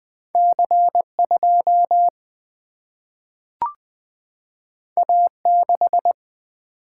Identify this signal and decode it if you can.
Morse code, heard as a keyed tone, about 20 wpm, 700 Hz. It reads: C 2 A 6
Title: Single-Letter Single-Number 20wpm